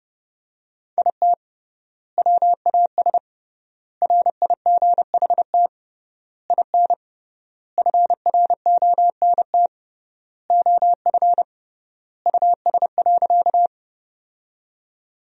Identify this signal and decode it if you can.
Morse code, heard as a keyed tone, about 32 wpm, 700 Hz. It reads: IT WAS RIGHT IN FRONT OF US.
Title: 3x Word-Spacing - Sentences from Top 500 Words - 30wpm